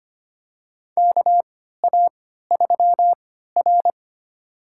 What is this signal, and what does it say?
Morse code, heard as a keyed tone, about 25 wpm, spaced as for 14 wpm, 700 Hz. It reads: KA3R
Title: USA Call Signs - 4 Characters - 3x Character Spacing - 25wpm